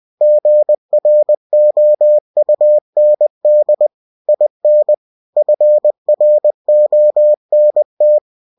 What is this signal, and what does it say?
Morse code, heard as a keyed tone, about 20 wpm, 600 Hz. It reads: GROUND IN FRONT
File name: winnie-20wpm